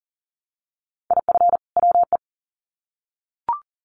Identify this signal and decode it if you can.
Morse code, heard as a keyed tone, about 40 wpm, 700 Hz. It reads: IF WE
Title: Sets of 2 Words 40wpm